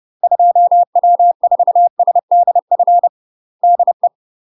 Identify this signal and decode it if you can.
Morse code, heard as a keyed tone, about 30 wpm, 700 Hz. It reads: 2W4SDF DE
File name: UK MASTER QSOs TRAINING/30WPM